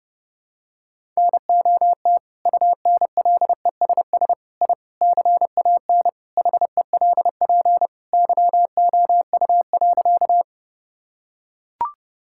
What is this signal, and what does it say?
Morse code, heard as a keyed tone, about 30 wpm, 700 Hz. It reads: NOT UNLESS I CAN HELP YOU.
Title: Sentences from Top 1000 Words - Review Part 6 - 30wpm